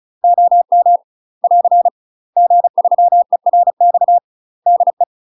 Morse code, heard as a keyed tone, about 36 wpm, 700 Hz. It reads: OM <AR> G3ERX DE
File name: UK MASTER QSOs TRAINING / 35WPM / UK MASTER QSOs 0041